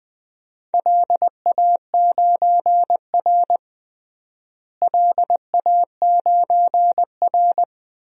Morse code, heard as a keyed tone, about 20 wpm, 700 Hz. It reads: LA9R LA9R
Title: DXCC-100 Call Signs - 4 Characters - Repeated 2x - 20wpm